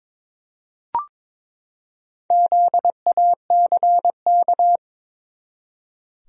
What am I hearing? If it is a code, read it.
Morse code, heard as a keyed tone, about 22 wpm, 700 Hz. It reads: ZACK
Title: USA Common Names - Max 5 Letters - 22wpm